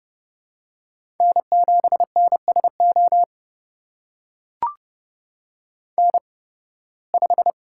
Morse code, heard as a keyed tone, about 32 wpm, spaced as for 20 wpm, 700 Hz. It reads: N7NSO N5
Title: USA Call Signs - 5 Characters - 8x Character Spacing - 30wpm